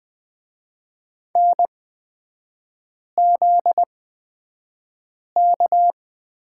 Morse code, heard as a keyed tone, about 20 wpm, spaced as for 5 wpm, 700 Hz. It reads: NZK
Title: 243 Lesson 39e - Letter z - Three Characters 20wpm